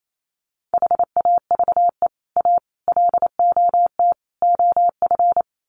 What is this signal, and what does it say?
Morse code, heard as a keyed tone, about 28 wpm, 700 Hz. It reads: HAVE A LOT OF
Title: Sets of 4 Words 28wpm